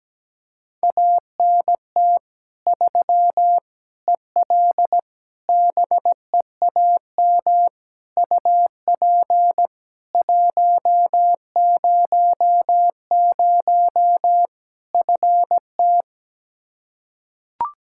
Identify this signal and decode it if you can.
Morse code, heard as a keyed tone, about 17 wpm, 700 Hz. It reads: ANT 3 EL BEAM UP 100 FT
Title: Practice QSO - Exchange 2 - Info - 17wpm